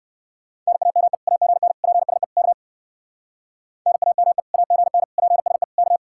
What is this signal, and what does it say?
Morse code, heard as a keyed tone, about 59 wpm, 700 Hz. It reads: NAME AGN PSE K NAME AGN PSE K